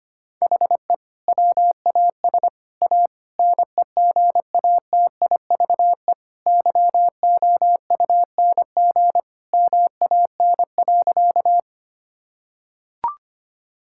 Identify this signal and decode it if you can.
Morse code, heard as a keyed tone, about 25 wpm, 700 Hz. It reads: HE WAS A NEGATIVE YOUNG MAN.
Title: Sentences from Top 1000 Words - Review Part 6 - 25wpm